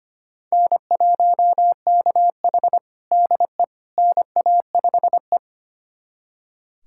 Morse code, heard as a keyed tone, about 25 wpm, 700 Hz. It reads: N1KH DE NA5E
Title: USA Callsign DE Callsign - 4 Characters - 25wpm